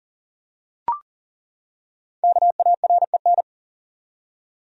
Morse code, heard as a keyed tone, about 40 wpm, 700 Hz. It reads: KAREN
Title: USA Common Names - Max 5 Letters - 40wpm